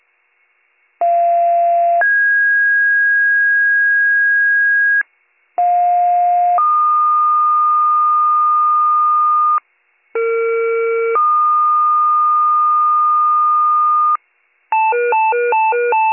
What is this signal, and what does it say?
Two-tone page, heard as a one-second tone, then a three-second tone, about 690 then 1650 Hz, about 710 then 1220 Hz, about 460 then 1220 Hz. It is heard from 1.0 to 14.2 s.